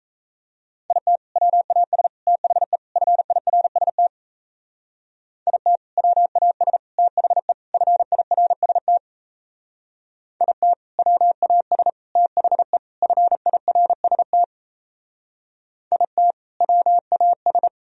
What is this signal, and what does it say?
Morse code, heard as a keyed tone, about 34 wpm, 700 Hz. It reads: IT WAS THE FIRST IT WAS THE FIRST IT WAS THE FIRST IT WAS
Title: Speed-Racing - Sets of 4 Words 28wpm